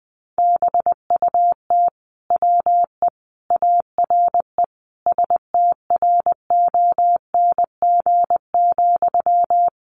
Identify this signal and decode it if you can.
Morse code, heard as a keyed tone, about 20 wpm, 700 Hz. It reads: BUT WE ARE STRONG,
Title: Sentences from Top 800 Words 20wpm